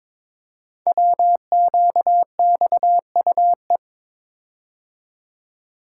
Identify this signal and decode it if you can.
Morse code, heard as a keyed tone, about 22 wpm, 700 Hz. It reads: WQXUE
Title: Code Groups - 5 Letters - 22wpm